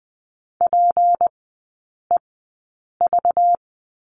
Morse code, heard as a keyed tone, about 20 wpm, spaced as for 8 wpm, 700 Hz. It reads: PEV
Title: 191 Lesson 31e - Letter v - Three Characters 20wpm